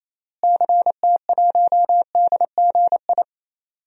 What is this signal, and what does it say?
Morse code, heard as a keyed tone, about 27 wpm, 700 Hz. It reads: CT1DGI
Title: DXCC-100 Call Signs - 6 Characters - Repeated 2x - 28wpm